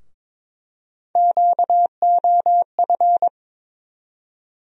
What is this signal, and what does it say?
Morse code, heard as a keyed tone, about 22 wpm, 700 Hz. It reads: QOF
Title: USA Call Sign Postfixes - 3 Letters 22wpm